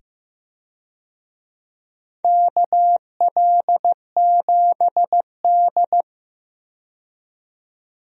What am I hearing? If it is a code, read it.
Morse code, heard as a keyed tone, about 15 wpm, 700 Hz. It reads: KL7D
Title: DXCC-100 Call Signs - 4 Characters - 15wpm